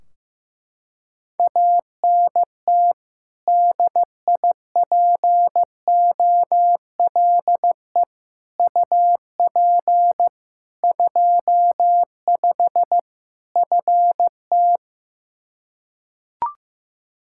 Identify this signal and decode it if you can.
Morse code, heard as a keyed tone, about 15 wpm, 700 Hz. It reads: ANT DIPOLE UP 25 FT
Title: Practice QSO - Exchange 2 - Info - 15wpm